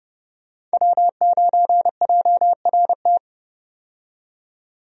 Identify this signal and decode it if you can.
Morse code, heard as a keyed tone, about 29 wpm, 700 Hz. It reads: W9JRT
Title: USA Call Signs - 5 Characters - 30wpm